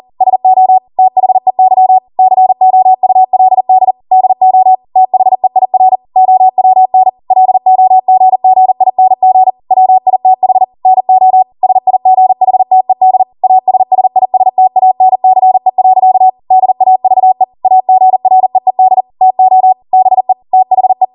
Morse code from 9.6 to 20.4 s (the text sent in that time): WITH NO SIGHTED ASSISTANCE. DAVE AGREED TO BE